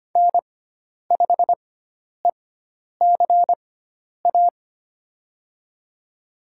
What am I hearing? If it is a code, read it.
Morse code, heard as a keyed tone, about 25 wpm, spaced as for 10 wpm, 700 Hz. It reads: N5ECA
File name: Lesson 16g - Letter c - Call Signs 25wpm